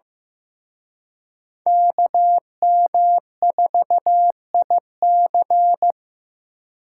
Morse code, heard as a keyed tone, about 18 wpm, 700 Hz. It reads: KM4IC
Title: Speed Racing - USA Call Signs 15wpm